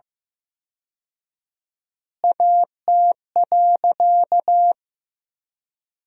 Morse code, heard as a keyed tone, about 15 wpm, 700 Hz. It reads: AT.
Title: 4x Word-Spacing - Sentences from Top 200 Words - 15wpm